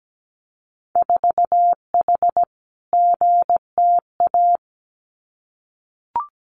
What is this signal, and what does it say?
Morse code, heard as a keyed tone, about 17 wpm, 700 Hz. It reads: 4H GTA
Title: Winter Field Day Exchanges Encore 17wpm